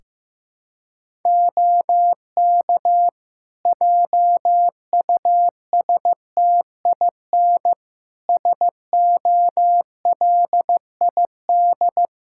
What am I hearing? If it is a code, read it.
Morse code, heard as a keyed tone, about 16 wpm, 700 Hz. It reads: OK JUSTIN SOLID
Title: Practice QSO - Exchange 3 - More Info - 15wpm